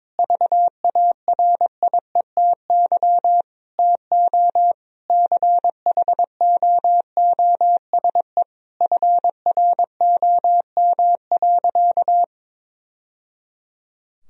Morse code, heard as a keyed tone, about 21 wpm, 700 Hz. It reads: VARIETY TO CHOOSE FROM.
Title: Sentences from Top 1000 Words - Review Part 4 - 22wpm